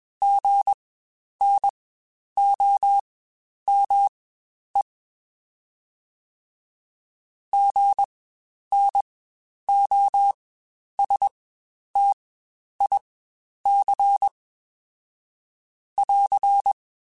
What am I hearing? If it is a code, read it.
Morse code, heard as a keyed tone, about 20 wpm, spaced as for 10 wpm, 800 Hz. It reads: GNOME GNOSTIC <AR>
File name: Spaced Alphas